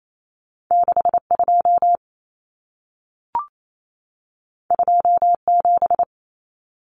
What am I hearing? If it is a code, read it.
Morse code, heard as a keyed tone, about 28 wpm, 700 Hz. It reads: 62 27
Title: Common Cut Numbers - 2-Digit 28wpm